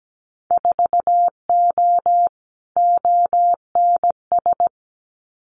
Morse code, heard as a keyed tone, about 17 wpm, 700 Hz. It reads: 4O ONS
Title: Winter Field Day Exchanges 17wpm